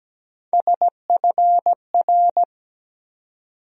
Morse code, heard as a keyed tone, about 17 wpm, 700 Hz. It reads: SFR
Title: USA Call Sign Postfixes - 3 Letters 17wpm